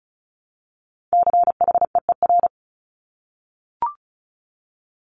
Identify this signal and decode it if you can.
Morse code, heard as a keyed tone, about 36 wpm, 700 Hz. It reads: CHEER
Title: Multi-letter Phonemes 35wpm